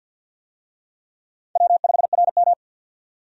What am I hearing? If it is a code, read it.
Morse code, heard as a keyed tone, about 52 wpm, 700 Hz. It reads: W5RM
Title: USA Call Signs - 4 Characters - 50wpm